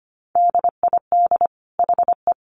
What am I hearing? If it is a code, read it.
Morse code, heard as a keyed tone, about 25 wpm, 700 Hz. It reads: DID HE
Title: Sentences from Top 1100 Words 25wpm